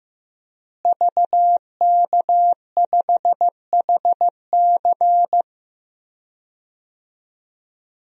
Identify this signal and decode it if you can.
Morse code, heard as a keyed tone, about 15 wpm, 700 Hz. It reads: VK5HC
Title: DXCC-100 Call Signs - 5 Characters - Repeated 3x - 15wpm